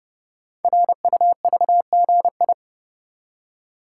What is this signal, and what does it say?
Morse code, heard as a keyed tone, about 30 wpm, 700 Hz. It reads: RUVGI
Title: Code Groups - 5 Letter:Numbers - 30wpm